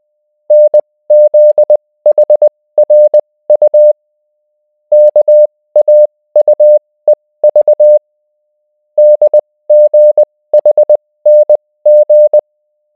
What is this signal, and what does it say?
Morse code, heard as a keyed tone, about 20 wpm, spaced as for 16 wpm, 600 Hz. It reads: NZHRU KAUEV DGHNG